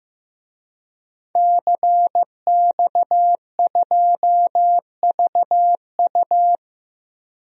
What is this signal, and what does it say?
Morse code, heard as a keyed tone, about 16 wpm, 700 Hz. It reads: CX2VU
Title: DXCC-100 Call Signs - 5 Characters - 15wpm